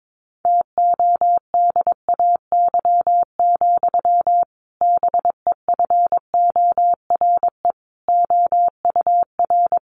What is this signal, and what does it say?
Morse code, heard as a keyed tone, about 22 wpm, 700 Hz. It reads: TODAY, BEFORE OUR
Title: A Sound Of Thunder 22wpm